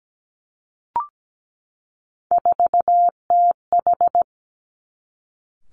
Morse code, heard as a keyed tone, about 17 wpm, 700 Hz. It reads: 4TH
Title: Three Challenging Letter-Number Combinations 17wpm